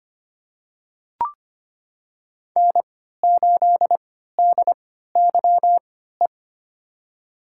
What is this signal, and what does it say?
Morse code, heard as a keyed tone, about 25 wpm, spaced as for 14 wpm, 700 Hz. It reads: N8DYE
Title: USA Call Signs - 5 Characters - 3x Character Spacing - 25wpm